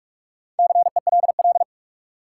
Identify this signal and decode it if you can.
Morse code, heard as a keyed tone, about 47 wpm, 700 Hz. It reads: CELL
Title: Four Letter Words - 45wpm